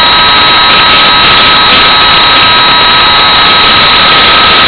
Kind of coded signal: PSK31